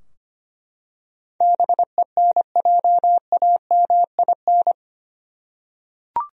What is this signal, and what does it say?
Morse code, heard as a keyed tone, about 25 wpm, 700 Hz. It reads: BENJAMIN
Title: USA Common Names - 25wpm